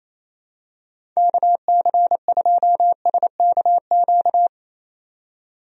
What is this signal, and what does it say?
Morse code, heard as a keyed tone, about 28 wpm, 700 Hz. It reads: KC2SKQ
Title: USA Call Signs - 6 Characters - 28wpm